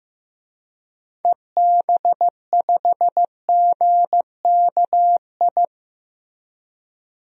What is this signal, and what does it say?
Morse code, heard as a keyed tone, about 15 wpm, 700 Hz. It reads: EB5GKI
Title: DXCC-100 Call Signs - 6 Characters - Repeated 2x - 15wpm